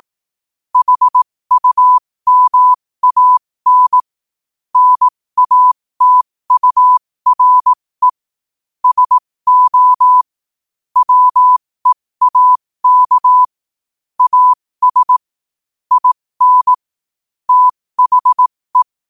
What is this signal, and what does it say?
Morse code, heard as a keyed tone, about 18 wpm, spaced as for 15 wpm, 1000 Hz. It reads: HUMAN NATURE SO WEAK AS IN THE
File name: qotdfiles / 15wpm